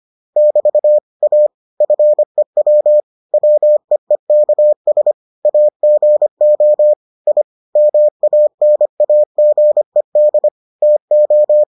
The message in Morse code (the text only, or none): <BT> A FEW WEEKS AGO I MANAGED TO